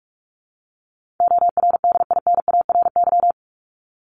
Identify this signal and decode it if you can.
Morse code, heard as a keyed tone, about 44 wpm, 700 Hz. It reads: ORDINARY
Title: Top 1751-2000 Words 45wpm